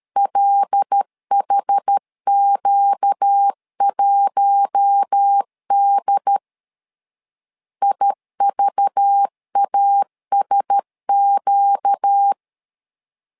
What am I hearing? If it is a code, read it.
Morse code, heard as a keyed tone, about 13 wpm, 800 Hz. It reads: LHQ1D IVASQ
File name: LEZIONI / 5 LBVC / 5 LBV 12wpm 1855